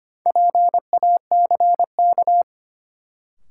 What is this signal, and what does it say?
Morse code, heard as a keyed tone, about 25 wpm, 700 Hz. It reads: PACK
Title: Sentences from Top 1300 Words - New Word Drill - 25wpm